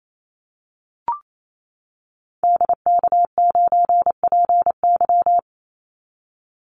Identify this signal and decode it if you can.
Morse code, heard as a keyed tone, about 28 wpm, 700 Hz. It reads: DK9PY
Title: Call Signs - 28wpm